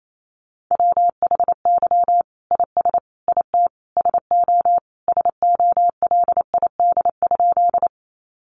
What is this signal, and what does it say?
Morse code, heard as a keyed tone, about 28 wpm, 700 Hz. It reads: WHY IS IT SO SOLID?
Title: Sentences from Top 1500 Words 28wpm